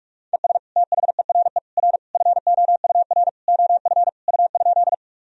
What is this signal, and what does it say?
Morse code, heard as a keyed tone, about 47 wpm, 700 Hz. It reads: ES THERE R FOUR OF U?
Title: Sentences from Top 300 Words and Abbreviations 45wpm